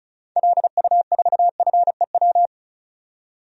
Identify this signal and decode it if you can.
Morse code, heard as a keyed tone, about 34 wpm, 700 Hz. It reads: LU4FEW
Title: DXCC-100 Call Signs - 6 Characters - 35wpm